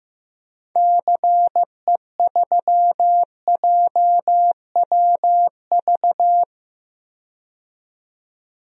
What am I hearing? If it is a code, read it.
Morse code, heard as a keyed tone, about 16 wpm, 700 Hz. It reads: CE3JWV
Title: DXCC-100 Call Signs - 6 Characters - 15wpm